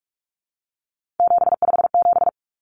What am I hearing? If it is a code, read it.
Morse code, heard as a keyed tone, about 44 wpm, 700 Hz. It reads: 757
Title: Common Cut Numbers - 3-Digit 45wpm